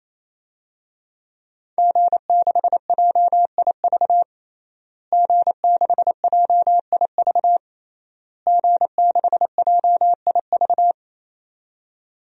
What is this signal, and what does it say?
Morse code, heard as a keyed tone, about 28 wpm, 700 Hz. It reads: G6JIV G6JIV G6JIV
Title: DXCC-100 Call Signs - 5 Characters - Repeated 3x - 28wpm